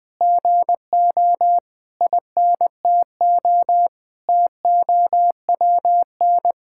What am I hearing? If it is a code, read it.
Morse code, heard as a keyed tone, about 19 wpm, 700 Hz. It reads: GO INTO TOWN
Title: Sentences from Top 1000 Words - Review Part 6 - 20wpm